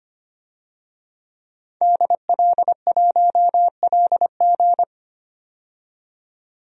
Morse code, heard as a keyed tone, about 25 wpm, 700 Hz. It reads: DL1LG
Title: DXCC-100 Call Signs - 5 Characters - 25wpm